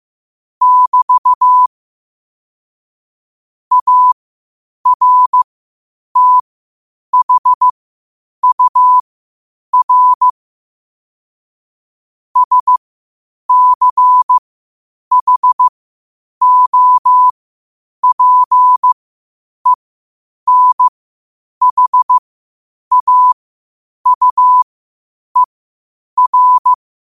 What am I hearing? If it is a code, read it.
Morse code, heard as a keyed tone, about 15 wpm, spaced as for 8 wpm, 1000 Hz. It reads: <BT> ARTHUR SCHOPENHAUER
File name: qotdfiles / 8wpm